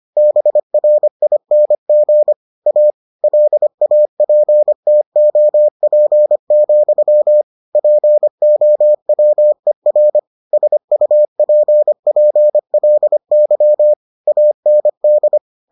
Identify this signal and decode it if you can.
Morse code, heard as a keyed tone, about 25 wpm, 600 Hz. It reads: BRING A LAPTOP, POWER SUPPLY AND